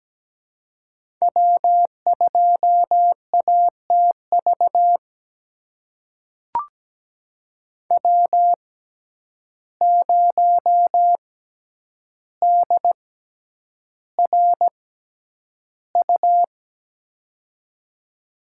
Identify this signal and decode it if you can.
Morse code, heard as a keyed tone, about 17 wpm, spaced as for 9 wpm, 700 Hz. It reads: W2ATV W0DRU
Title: USA Call Signs - 5 Characters - 6x Character Spacing - 17wpm